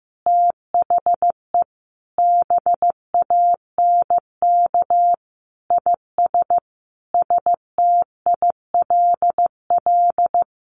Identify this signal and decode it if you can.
Morse code, heard as a keyed tone, about 15 wpm, 700 Hz. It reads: THE BANK IS STILL
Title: Sentences from Top 1500 Words 15wpm